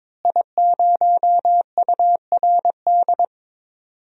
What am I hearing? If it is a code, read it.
Morse code, heard as a keyed tone, about 22 wpm, 700 Hz. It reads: I0URD
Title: DXCC-100 Call Signs - 5 Characters - Repeated 3x - 22wpm